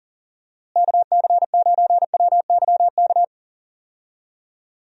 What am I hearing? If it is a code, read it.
Morse code, heard as a keyed tone, about 38 wpm, 700 Hz. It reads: KC9WYK
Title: USA Call Signs - 6 Characters - Repeated 2x - 40wpm